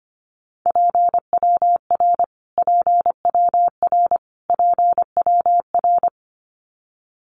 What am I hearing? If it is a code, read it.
Morse code, heard as a keyed tone, about 25 wpm, 700 Hz. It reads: PWR PWR PWR
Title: Abbreviations Repeated 3x - 25wpm